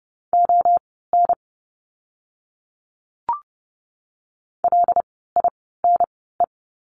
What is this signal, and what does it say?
Morse code, heard as a keyed tone, about 30 wpm, spaced as for 17 wpm, 700 Hz. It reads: ON LINE
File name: course / 30:10wpm / 115 Lesson 19f - Letter w - Words 30wpm